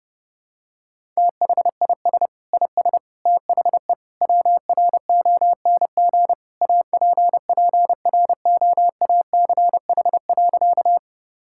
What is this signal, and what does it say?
Morse code, heard as a keyed tone, about 30 wpm, 700 Hz. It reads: THIS IS THE WRONG APPROACH.
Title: Sentences from Top 1000 Words - Review Part 6 - 30wpm